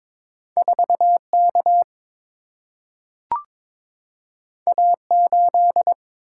Morse code, heard as a keyed tone, about 22 wpm, 690 Hz. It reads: 4K A8
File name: practice / 22wpm / Two Letter-Number Combinations - 22wpm